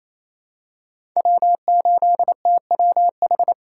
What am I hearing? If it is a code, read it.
Morse code, heard as a keyed tone, about 28 wpm, 700 Hz. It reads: W8TWH
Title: USA Call Signs - 5 Characters - 4x Character Spacing - 28wpm